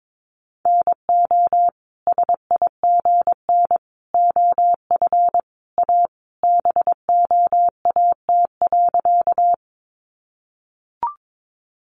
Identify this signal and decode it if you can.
Morse code, heard as a keyed tone, about 22 wpm, 700 Hz. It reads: NO SIGN OF A BOAT.
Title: Sentences from Top 1000 Words 22wpm